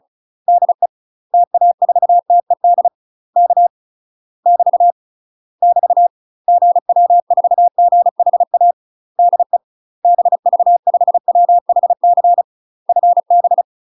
Morse code, heard as a keyed tone, about 35 wpm, 700 Hz. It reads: DE TA4TED K <BT> <BT> GW4GHA DE BV5WHC FB
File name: UK MASTER QSOs TRAINING / 35WPM / UK MASTER QSOs 0015